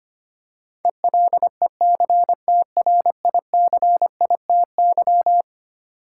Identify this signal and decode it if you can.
Morse code, heard as a keyed tone, about 25 wpm, 700 Hz. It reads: ELECTRICITY
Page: morse-code-ninja-practice-25wpm